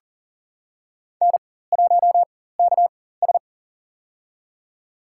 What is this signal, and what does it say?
Morse code, heard as a keyed tone, about 40 wpm, spaced as for 19 wpm, 700 Hz. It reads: N1KS
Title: USA Call Signs - 4 Characters - 4x Character Spacing - 40wpm